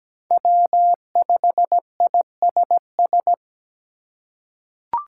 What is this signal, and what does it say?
Morse code, heard as a keyed tone, about 17 wpm, 700 Hz. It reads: W5ISS
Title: USA Call Signs - 8x Character Spacing - 17wpm